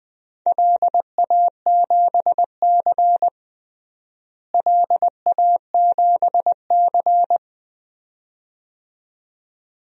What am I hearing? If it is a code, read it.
Morse code, heard as a keyed tone, about 20 wpm, 700 Hz. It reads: LA7C LA7C
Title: DXCC-100 Call Signs - 4 Characters - Repeated 2x - 20wpm